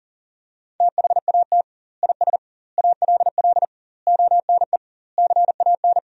Morse code, heard as a keyed tone, about 40 wpm, 700 Hz. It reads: THAT IS ALL ONE CAN
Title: 2x Word-Spacing - Sentences from Top 200 Words - 40wpm